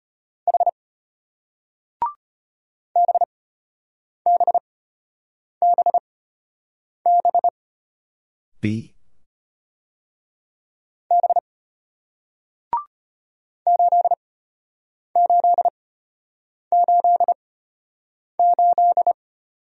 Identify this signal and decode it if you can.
Morse code, heard as a keyed tone, about 33 wpm, spaced as for 8 wpm, 700 Hz. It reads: H BBBB B 8888